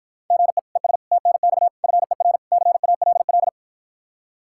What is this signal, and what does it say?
Morse code, heard as a keyed tone, about 52 wpm, 700 Hz. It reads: GE ES TNX FER CALL